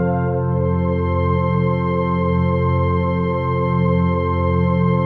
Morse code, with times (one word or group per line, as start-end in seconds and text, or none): none